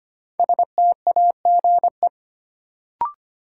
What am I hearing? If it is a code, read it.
Morse code, heard as a keyed tone, about 25 wpm, 700 Hz. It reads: STAGE
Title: Sentences from Top 1200 Words - New Word Drill Alt - 25wpm